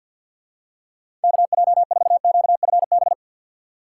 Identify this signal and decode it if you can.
Morse code, heard as a keyed tone, about 49 wpm, 700 Hz. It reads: KJ4QFB
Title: USA Call Signs - 6 Characters - 3x Character Spacing - 50wpm